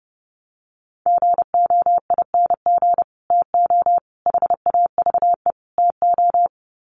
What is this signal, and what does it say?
Morse code, heard as a keyed tone, about 28 wpm, 700 Hz. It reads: GOING TO HAVE TO
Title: Sets of 4 Words 30wpm